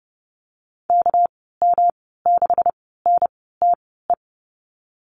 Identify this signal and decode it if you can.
Morse code, heard as a keyed tone, about 30 wpm, spaced as for 17 wpm, 700 Hz. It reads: KM6NTE
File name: Lesson 37g - Letter x - Call Signs 30wpm